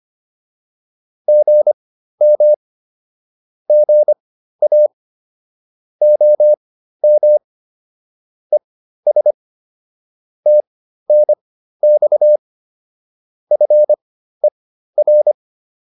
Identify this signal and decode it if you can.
Morse code, heard as a keyed tone, about 25 wpm, spaced as for 13 wpm, 600 Hz. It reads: GM GA OM ES TNX FER